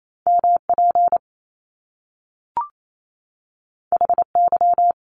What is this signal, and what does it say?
Morse code, heard as a keyed tone, about 28 wpm, 700 Hz. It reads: MP HY
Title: Top Two Letter Combinations in Words 28wpm